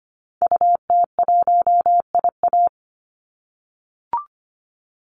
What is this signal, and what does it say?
Morse code, heard as a keyed tone, about 25 wpm, 700 Hz. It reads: UT1IA
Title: Call Signs - 25wpm